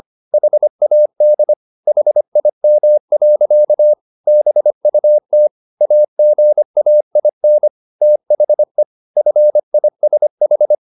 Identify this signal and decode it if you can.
Morse code, heard as a keyed tone, about 25 wpm, 600 Hz. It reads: HAD HIM. BUT AGAIN THE FISH